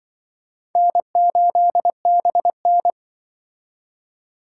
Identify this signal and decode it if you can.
Morse code, heard as a keyed tone, about 24 wpm, 700 Hz. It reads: N8BN
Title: Speed Racing - USA Call Signs 20wpm